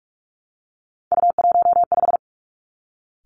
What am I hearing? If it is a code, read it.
Morse code, heard as a keyed tone, about 46 wpm, 700 Hz. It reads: U15
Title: Three Challenging Letter-Number Combinations 45wpm